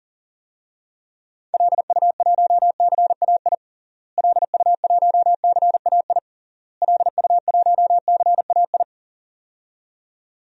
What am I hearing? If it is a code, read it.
Morse code, heard as a keyed tone, about 40 wpm, 700 Hz. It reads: LU1CAI LU1CAI LU1CAI
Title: DXCC-100 Call Signs - 6 Characters - Repeated 3x - 40wpm